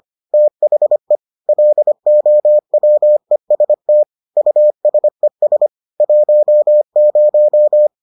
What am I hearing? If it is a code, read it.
Morse code, heard as a keyed tone, about 25 wpm, 600 Hz. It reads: THE LOWEST USES 10